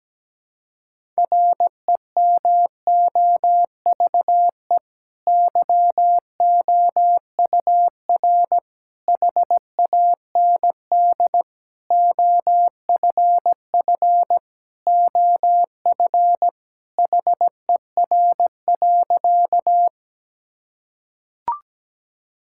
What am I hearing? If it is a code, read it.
Morse code, heard as a keyed tone, about 17 wpm, 700 Hz. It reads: REMOVE YOUR HAND OFF OF HER.
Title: Sentences from Top 1000 Words - Review Part 5 - 17wpm